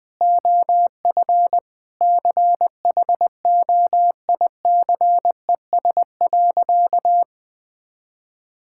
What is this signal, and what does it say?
Morse code, heard as a keyed tone, about 20 wpm, 700 Hz. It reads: OF CHOICES.
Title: Sentences with 3-Word Phrases - 20wpm